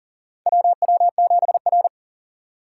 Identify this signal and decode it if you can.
Morse code, heard as a keyed tone, about 40 wpm, 700 Hz. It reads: WW7R
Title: USA Call Signs - 4 Characters - 2x Character Spacing - 40wpm